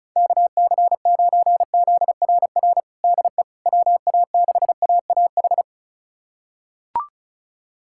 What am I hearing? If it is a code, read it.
Morse code, heard as a keyed tone, about 34 wpm, 700 Hz. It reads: KC9ZRR DE WA6AAH
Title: USA Callsign DE Callsign 35wpm